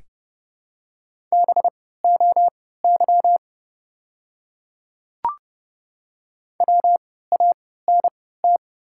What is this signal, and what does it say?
Morse code, heard as a keyed tone, about 31 wpm, spaced as for 17 wpm, 700 Hz. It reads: BOY WANT
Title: 3x Character Spacing - Top 200 Words - 30wpm